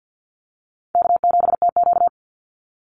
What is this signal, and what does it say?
Morse code, heard as a keyed tone, about 50 wpm, 700 Hz. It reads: K7TQ
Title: NA SOTA Chaser Callsigns 50wpm